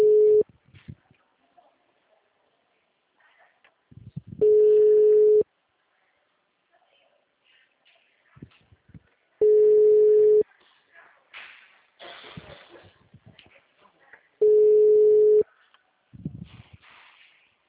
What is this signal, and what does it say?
Ringback tone (Europe, 425 Hz), with speech.
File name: out- -2002- - - .51168